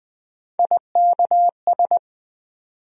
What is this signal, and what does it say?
Morse code, heard as a keyed tone, about 20 wpm, 700 Hz. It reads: IKS
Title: USA Call Sign Postfixes - 3 Letters 20wpm